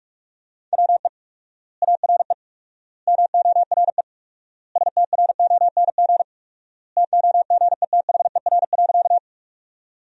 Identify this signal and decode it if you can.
Morse code, heard as a keyed tone, about 46 wpm, 700 Hz. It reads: WE ARE MORE STRONG TOGETHER.
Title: 4x Word-Spacing - Sentences from Top 500 Words - 45wpm